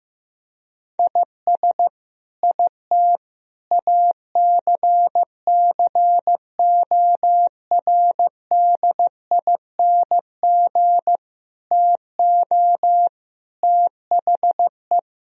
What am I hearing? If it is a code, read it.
Morse code, heard as a keyed tone, about 15 wpm, 700 Hz. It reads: IS IT ACCORDING TO THE
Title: Sentences from Top 1000 Words - Review Part 4 - 15wpm